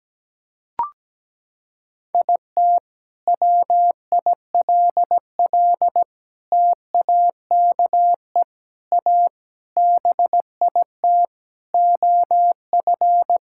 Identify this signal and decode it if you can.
Morse code, heard as a keyed tone, about 17 wpm, 700 Hz. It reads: IT WILL TAKE A BIT OF
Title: Sentences from Top 1000 Words - Review Part 5 - 17wpm